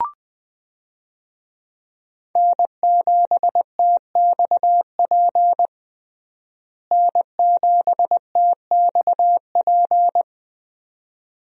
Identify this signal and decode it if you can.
Morse code, heard as a keyed tone, about 20 wpm, 700 Hz. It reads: N7TXP N7TXP
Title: USA Call Signs - 5 Characters - Repeated 2x - 20wpm